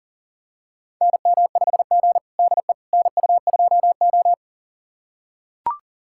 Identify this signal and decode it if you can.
Morse code, heard as a keyed tone, about 40 wpm, 700 Hz. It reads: NM5G DE NU2O
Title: USA Callsign DE Callsign - 4 Characters - 40wpm